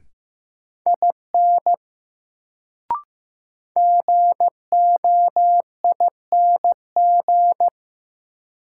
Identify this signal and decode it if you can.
Morse code, heard as a keyed tone, about 15 wpm, 700 Hz. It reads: IN GOING
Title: Sets of 2 Words - New Word Drill - 15wpm